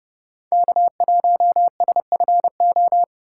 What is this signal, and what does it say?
Morse code, heard as a keyed tone, about 33 wpm, 700 Hz. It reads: K1SFO